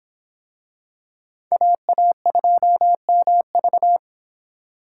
Morse code, heard as a keyed tone, about 22 wpm, 700 Hz. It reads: AA2MV